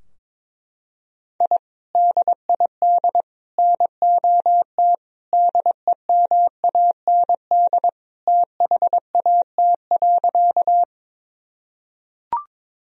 Morse code, heard as a keyed tone, about 22 wpm, 700 Hz. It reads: I DID NOT DEMAND THAT.
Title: Sentences from Top 1000 Words - Review Part 4 - 22wpm